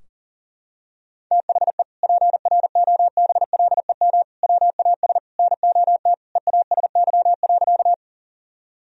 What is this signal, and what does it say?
Morse code, heard as a keyed tone, about 40 wpm, 700 Hz. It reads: THE PROBLEM WAS NOT EASY.
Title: Sentences from Top 1000 Words - Review Part 2 - 40wpm